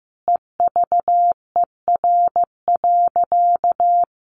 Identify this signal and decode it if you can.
Morse code, heard as a keyed tone, about 15 wpm, 700 Hz. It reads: EVER.
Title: Sentences from Top 1300 Words 15wpm